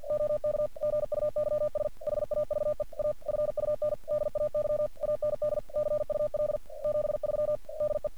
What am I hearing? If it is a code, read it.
Morse code, heard as a keyed tone, about 48 wpm, 600 Hz. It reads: OK GUYS HAVE A FUN DAY AND CUL 73 DE
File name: archived / CRSnet